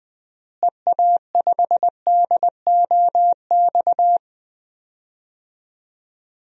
Morse code, heard as a keyed tone, about 20 wpm, 700 Hz. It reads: EA5DOX
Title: DXCC-100 Call Signs - 6 Characters - Repeated 2x - 20wpm